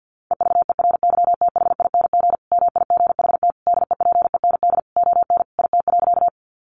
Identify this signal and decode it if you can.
Morse code, heard as a keyed tone, about 50 wpm, 700 Hz. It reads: EVERYTHING MIGHT DEPEND ON IT.
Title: Sentences from Top 1300 Words 50wpm